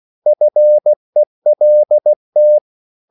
Morse code, heard as a keyed tone, about 16 wpm, 600 Hz. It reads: FELT